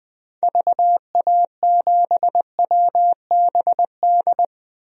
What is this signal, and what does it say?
Morse code, heard as a keyed tone, about 20 wpm, 700 Hz. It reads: VA7WBD
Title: DXCC-100 Call Signs - 6 Characters - 20wpm